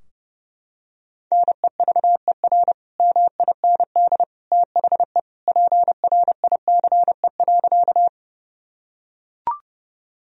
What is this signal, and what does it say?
Morse code, heard as a keyed tone, about 32 wpm, 700 Hz. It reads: NEVER MIND THE PRICE.
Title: Sentences from Top 1000 Words - Review Part 4 - 30wpm